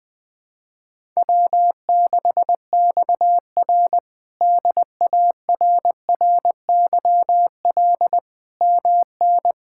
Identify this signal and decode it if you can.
Morse code, heard as a keyed tone, about 19 wpm, 700 Hz. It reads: W6XR DARRYL MN
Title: K1USN SST QSO Practice - 20wpm